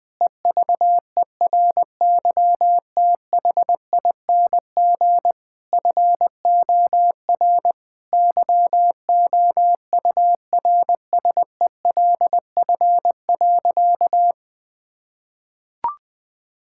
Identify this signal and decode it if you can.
Morse code, heard as a keyed tone, about 21 wpm, 700 Hz. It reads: EVERYTHING FOR YOURSELF.
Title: Sentences from Top 1000 Words - Review Part 6 - 20wpm